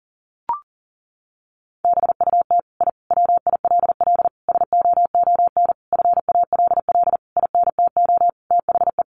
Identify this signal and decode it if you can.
Morse code, heard as a keyed tone, about 40 wpm, 700 Hz. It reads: BUT I WILL SOON FALL INTO THE
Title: Sentences from Top 1300 Words 40wpm